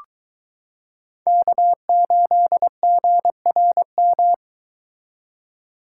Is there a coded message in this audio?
Yes: Morse code from 1.3 to 4.3 s, heard as a keyed tone, about 22 wpm, 700 Hz. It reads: K8GRM